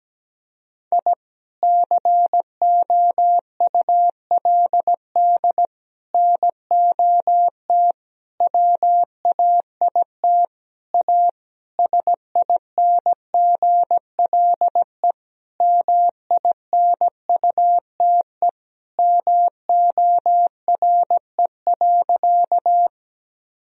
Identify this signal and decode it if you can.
Morse code, heard as a keyed tone, about 17 wpm, 700 Hz. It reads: I COULD NOT WAIT A SINGLE MINUTE MORE.
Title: Sentences from Top 1000 Words - Review Part 4 - 17wpm